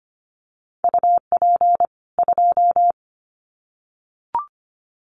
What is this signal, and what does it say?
Morse code, heard as a keyed tone, about 25 wpm, 700 Hz. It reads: UP 2
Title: Common QSO Phrases 25wpm